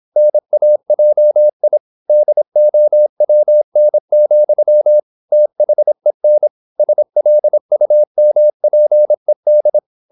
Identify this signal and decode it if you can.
Morse code, heard as a keyed tone, about 26 wpm, 600 Hz. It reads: NAJI DOWN, THEN SLUMPED
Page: Never 26wpm | CW med Gnister